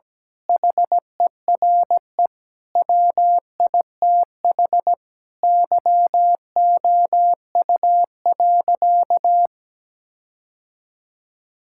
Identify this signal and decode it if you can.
Morse code, heard as a keyed tone, about 17 wpm, 700 Hz. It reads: HERE WITH YOU.
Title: Sentences from Top 1600 Words 17wpm